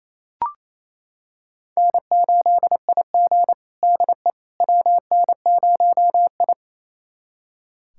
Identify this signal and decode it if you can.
Morse code, heard as a keyed tone, about 27 wpm, 690 Hz. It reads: N8IG DE WN0I
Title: USA Callsign DE Callsign - 4 Characters - 28wpm